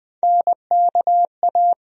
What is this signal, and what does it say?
Morse code, heard as a keyed tone, about 20 wpm, 700 Hz. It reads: NKA
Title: USA Call Sign Postfixes - 3 Letters 20wpm